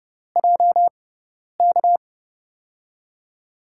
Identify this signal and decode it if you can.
Morse code, heard as a keyed tone, about 30 wpm, spaced as for 10 wpm, 700 Hz. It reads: JK
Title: USA Call Signs - 6 Characters - 6x Character Spacing - 30wpm